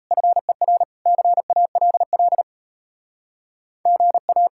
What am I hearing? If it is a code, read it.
Morse code, heard as a keyed tone, about 37 wpm, 700 Hz. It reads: FER CALL GA